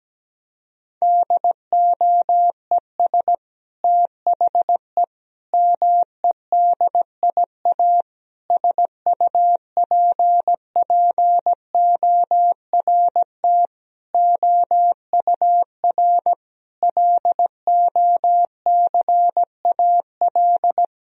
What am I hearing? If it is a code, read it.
Morse code, heard as a keyed tone, about 17 wpm, 700 Hz. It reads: DOES THE MEDIA SUPPORT OUR LOCAL
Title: Sentences from Top 1000 Words - Review Part 4 - 17wpm